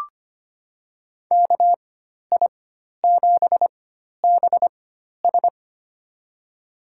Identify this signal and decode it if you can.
Morse code, heard as a keyed tone, about 25 wpm, spaced as for 12 wpm, 700 Hz. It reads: KI7BS
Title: USA Call Signs - 5 Characters - 4x Character Spacing - 25wpm